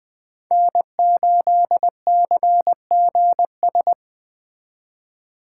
Morse code, heard as a keyed tone, about 19 wpm, 700 Hz. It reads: N8CGS